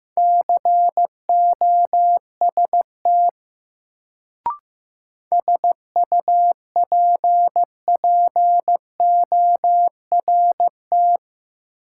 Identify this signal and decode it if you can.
Morse code, heard as a keyed tone, about 15 wpm, 700 Hz. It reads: COST SUPPORT
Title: Sentences from Top 800 Words - New Word Drill - 15wpm